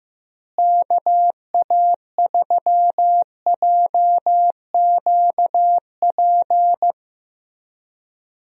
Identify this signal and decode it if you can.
Morse code, heard as a keyed tone, about 15 wpm, 700 Hz. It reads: KA3JQP
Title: USA Call Signs - 6 Characters - Repeated 3x - 15wpm